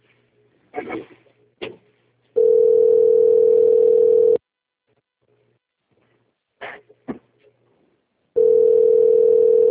Ringback tone (North America), 440 + 480 Hz.